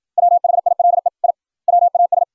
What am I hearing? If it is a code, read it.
Morse code, heard as a keyed tone, about 56 wpm, 700 Hz. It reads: WHILE I WAS